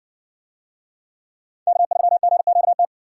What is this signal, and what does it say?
Morse code, heard as a keyed tone, about 60 wpm, 700 Hz. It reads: K3GQT